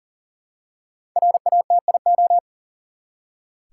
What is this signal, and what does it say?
Morse code, heard as a keyed tone, about 38 wpm, 700 Hz. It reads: RATIO
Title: Five Letter Words - 40wpm